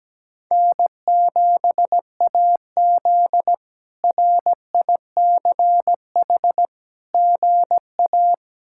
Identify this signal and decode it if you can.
Morse code, heard as a keyed tone, about 17 wpm, 700 Hz. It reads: N7AZ RICH GA
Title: K1USN SST QSO Practice - 17wpm